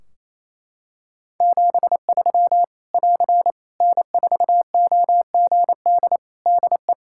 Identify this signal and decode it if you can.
Morse code, heard as a keyed tone, about 28 wpm, 700 Hz. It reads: 73 <AR> N4OGD DE
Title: K1USN SST QSO Practice 28wpm